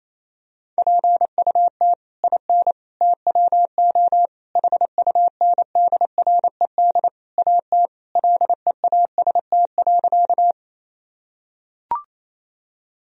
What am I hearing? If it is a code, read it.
Morse code, heard as a keyed tone, about 28 wpm, 700 Hz. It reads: PUT IN TWO HUNDRED AT LEAST.
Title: Sentences from Top 1000 Words - Review Part 2 - 28wpm